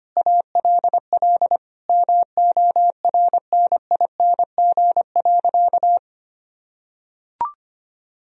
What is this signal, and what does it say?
Morse code, heard as a keyed tone, about 25 wpm, 700 Hz. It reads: ALL MORNING.
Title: Sentences from Top 1000 Words - Review Part 3 - 25wpm